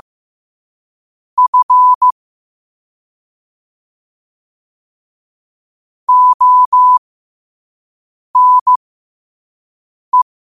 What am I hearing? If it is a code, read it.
Morse code, heard as a keyed tone, about 15 wpm, spaced as for 5 wpm, 1000 Hz. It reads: F ONE
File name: 5wpm